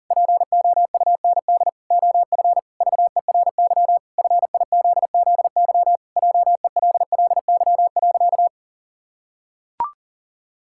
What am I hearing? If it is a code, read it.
Morse code, heard as a keyed tone, about 40 wpm, 700 Hz. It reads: POUND OF VERY FIZZY JELLY.
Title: Pangram Challenge v2 - 40wpm